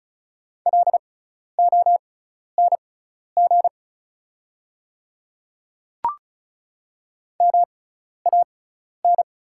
Morse code, heard as a keyed tone, about 36 wpm, spaced as for 12 wpm, 700 Hz. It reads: LONG MAN
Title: 6x Character Spacing - Top 200 Words - 35wpm